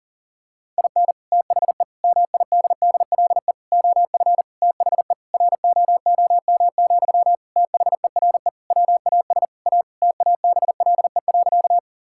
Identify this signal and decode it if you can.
Morse code, heard as a keyed tone, about 40 wpm, 700 Hz. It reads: IN THE MIDDLE OF THE ROOM, THERE WAS A TABLE.
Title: Sentences from Top 1000 Words - Review Part 6 - 40wpm